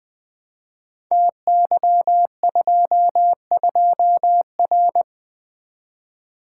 Morse code, heard as a keyed tone, about 20 wpm, 700 Hz. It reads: TY22R
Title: Code Groups - 5 Letter:Numbers - 20wpm